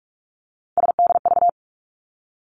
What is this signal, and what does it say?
Morse code, heard as a keyed tone, about 46 wpm, 700 Hz. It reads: SDV
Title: Three Challenging Letter-Number Combinations 45wpm